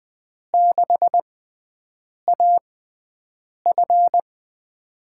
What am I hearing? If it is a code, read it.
Morse code, heard as a keyed tone, about 20 wpm, spaced as for 7 wpm, 700 Hz. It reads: 6AF
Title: USA Call Signs - 5 Characters - 6x Character Spacing - 20wpm